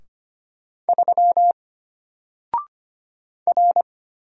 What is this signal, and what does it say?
Morse code, heard as a keyed tone, about 25 wpm, 700 Hz. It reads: 3 R
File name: Single-Letter-Number-25wpm